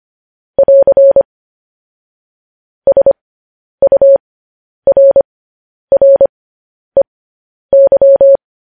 Morse code, heard as a keyed tone, about 25 wpm, spaced as for 10 wpm, 550 Hz. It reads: <AR> SURREY
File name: podcast / News headlines-1x-25x10